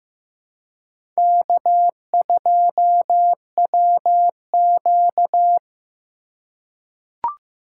Morse code, heard as a keyed tone, about 15 wpm, 700 Hz. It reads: K2WQ
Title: USA Call Signs - 2x Character Spacing - 15wpm